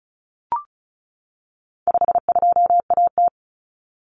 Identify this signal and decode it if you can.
Morse code, heard as a keyed tone, about 34 wpm, 700 Hz. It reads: 52AT
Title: Common Cut Numbers - 4-Digit 35wpm